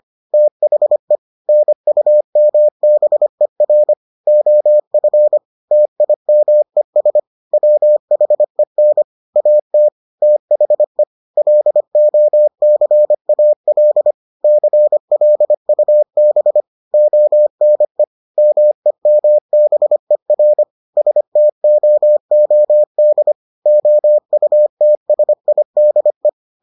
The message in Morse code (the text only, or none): THE NUMBER OF TIMES WHEN AT THE LOCAL CLUB ONE MEMBER STOOD OUTSIDE